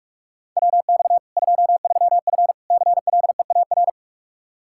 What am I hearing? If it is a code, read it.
Morse code, heard as a keyed tone, about 46 wpm, 700 Hz. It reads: WX 23F CLEAR
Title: K1USN SST QSO Practice 45wpm